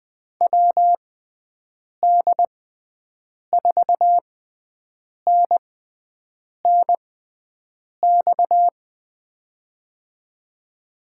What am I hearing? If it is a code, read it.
Morse code, heard as a keyed tone, about 20 wpm, spaced as for 7 wpm, 700 Hz. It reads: WD4NNX
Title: USA Call Signs - 6 Characters - 6x Character Spacing - 20wpm